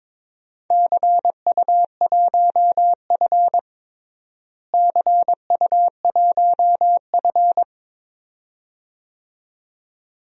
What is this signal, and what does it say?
Morse code, heard as a keyed tone, about 22 wpm, 700 Hz. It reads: CU1F CU1F
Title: DXCC-100 Call Signs - 4 Characters - Repeated 2x - 22wpm